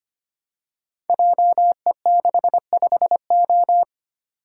Morse code, heard as a keyed tone, about 25 wpm, 700 Hz. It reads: JE65O
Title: Code Groups - 5 Letter:Numbers - 25wpm